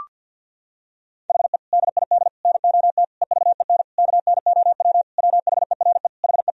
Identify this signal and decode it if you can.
Morse code, heard as a keyed tone, about 50 wpm, 700 Hz. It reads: HE DID NOT EVEN KNOW WHERE HE
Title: Sentences from Top 1000 Words - Review Part 1 - 50wpm